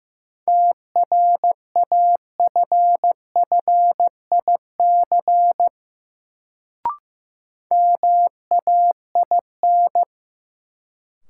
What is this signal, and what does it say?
Morse code, heard as a keyed tone, about 15 wpm, 700 Hz. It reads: TRAFFIC MAIN
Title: Sentences from Top 1300 Words - New Word Drill Alt - 15wpm